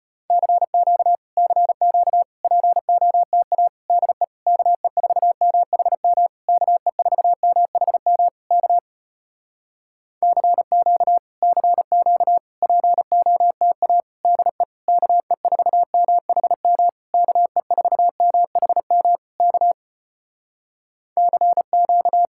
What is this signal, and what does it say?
Morse code, heard as a keyed tone, about 35 wpm, 700 Hz. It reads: CQ CQ POTA DE KE4MHM KE4MHM K CQ CQ POTA DE KE4MHM KE4MHM K CQ